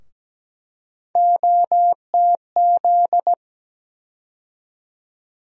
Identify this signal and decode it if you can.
Morse code, heard as a keyed tone, about 17 wpm, 700 Hz. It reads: OTZ
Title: USA Call Sign Postfixes - 3 Letters 17wpm